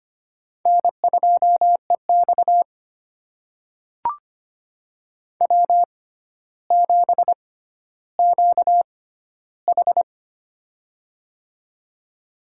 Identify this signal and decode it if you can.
Morse code, heard as a keyed tone, about 25 wpm, spaced as for 13 wpm, 700 Hz. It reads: N2EX W7QH
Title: USA Call Signs - 4 Characters - 6x Character Spacing - 25wpm